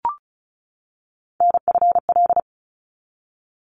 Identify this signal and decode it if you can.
Morse code, heard as a keyed tone, about 34 wpm, 700 Hz. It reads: NFL
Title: ARRL Sections 35wpm